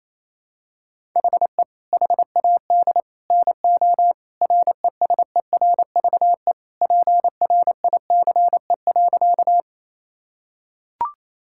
Morse code, heard as a keyed tone, about 28 wpm, 700 Hz. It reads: HE HAD NO RESERVE PRICE.
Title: Sentences from Top 1000 Words - Review Part 6 - 28wpm